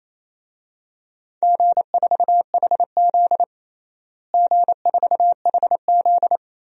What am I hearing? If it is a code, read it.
Morse code, heard as a keyed tone, about 28 wpm, 700 Hz. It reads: G4HZ G4HZ
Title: DXCC-100 Call Signs - 4 Characters - Repeated 3x - 28wpm